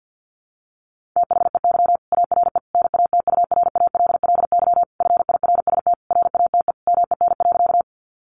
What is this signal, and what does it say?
Morse code, heard as a keyed tone, about 50 wpm, 700 Hz. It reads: THEY ARE NATURALLY FIRST RATE MEN.
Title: Sentences from Top 1300 Words 50wpm